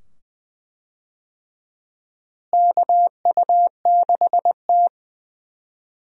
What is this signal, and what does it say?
Morse code, heard as a keyed tone, about 20 wpm, 700 Hz. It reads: KU6T
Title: USA Call Signs - 4 Characters - Repeated 2x - 20wpm